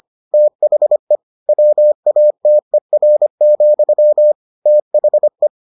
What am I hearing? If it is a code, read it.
Morse code, heard as a keyed tone, about 25 wpm, 600 Hz. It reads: THE WATER, THE